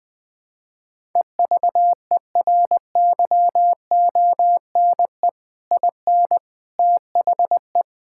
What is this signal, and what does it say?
Morse code, heard as a keyed tone, about 21 wpm, 700 Hz. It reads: EVERYONE IN THE
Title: Sentences with 3-Word Phrases - 20wpm